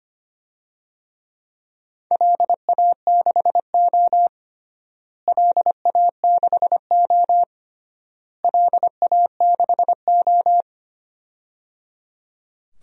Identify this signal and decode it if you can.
Morse code, heard as a keyed tone, about 25 wpm, 700 Hz. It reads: LA6O LA6O LA6O
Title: DXCC-100 Call Signs - 4 Characters - Repeated 3x - 25wpm